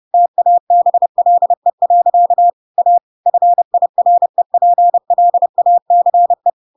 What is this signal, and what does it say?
Morse code, heard as a keyed tone, about 30 wpm, 700 Hz. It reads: TABLE. A FIREPLACE
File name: Pinocchio / Pinocchio ch3 30 30 ik1hge